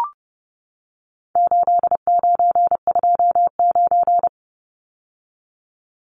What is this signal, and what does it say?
Morse code, heard as a keyed tone, about 30 wpm, 700 Hz. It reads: 8929
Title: Numbers - 4-Digit 30wpm